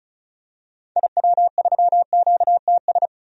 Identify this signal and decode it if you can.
Morse code, heard as a keyed tone, about 34 wpm, 700 Hz. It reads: IW3QTS
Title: DXCC-100 Call Signs - 6 Characters - 35wpm